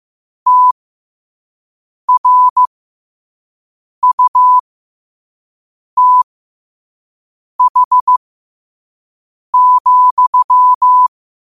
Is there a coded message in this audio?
Yes: Morse code from 0.5 to 11.1 s, heard as a keyed tone, about 15 wpm, spaced as for 5 wpm, 1000 Hz. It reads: TRUTH,